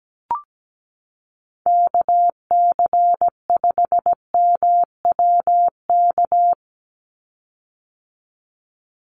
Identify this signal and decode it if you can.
Morse code, heard as a keyed tone, about 17 wpm, 700 Hz. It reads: KC5MWK
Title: USA Call Signs 17wpm